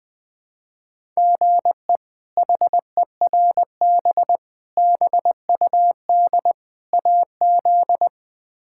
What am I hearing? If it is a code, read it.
Morse code, heard as a keyed tone, about 20 wpm, 700 Hz. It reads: GE HERB BUD AZ
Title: K1USN SST QSO Practice - 20wpm